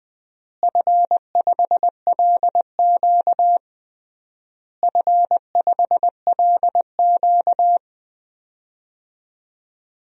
Morse code, heard as a keyed tone, about 20 wpm, 700 Hz. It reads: F5LQ F5LQ
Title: DXCC-100 Call Signs - 4 Characters - Repeated 3x - 20wpm